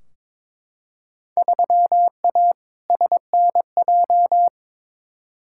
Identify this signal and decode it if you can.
Morse code, heard as a keyed tone, about 22 wpm, 700 Hz. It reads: 3A SNJ
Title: Field Day Exchanges - 22wpm